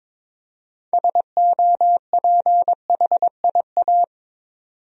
Morse code, heard as a keyed tone, about 22 wpm, 700 Hz. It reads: SOPHIA
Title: USA Common Names - 22wpm